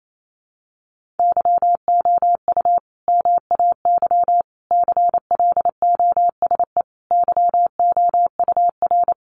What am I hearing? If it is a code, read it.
Morse code, heard as a keyed tone, about 28 wpm, 700 Hz. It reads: YOU MAY CLOSE YOUR
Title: Sentences from Top 1500 Words 28wpm